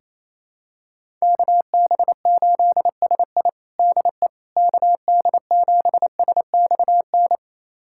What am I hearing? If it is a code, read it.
Morse code, heard as a keyed tone, about 28 wpm, 700 Hz. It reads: KB8SI DE KD7SXN
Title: USA Callsign DE Callsign 28wpm